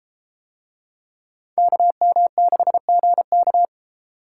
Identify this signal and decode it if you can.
Morse code, heard as a keyed tone, about 32 wpm, 700 Hz. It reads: KM6GK